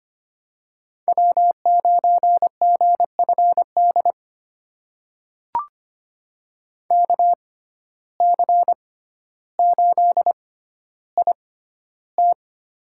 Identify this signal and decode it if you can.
Morse code, heard as a keyed tone, about 25 wpm, spaced as for 13 wpm, 700 Hz. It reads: W9GFD KC8IT
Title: USA Call Signs - 5 Characters - 6x Character Spacing - 25wpm